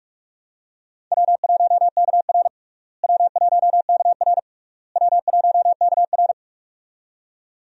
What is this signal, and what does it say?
Morse code, heard as a keyed tone, about 46 wpm, 700 Hz. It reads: W1KR W1KR W1KR
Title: USA Call Signs - 4 Characters - Repeated 3x - 45wpm